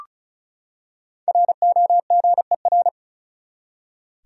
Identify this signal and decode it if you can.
Morse code, heard as a keyed tone, about 36 wpm, 700 Hz. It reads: ROGER
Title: USA Common Names - Max 5 Letters - 35wpm